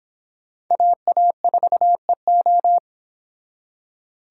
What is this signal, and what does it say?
Morse code, heard as a keyed tone, about 24 wpm, 700 Hz. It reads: AA4EO